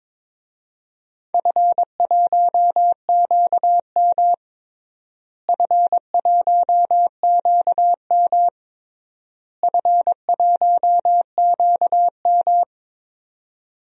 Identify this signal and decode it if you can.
Morse code, heard as a keyed tone, about 22 wpm, 700 Hz. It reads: F1QM F1QM F1QM
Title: DXCC-100 Call Signs - 4 Characters - Repeated 3x - 22wpm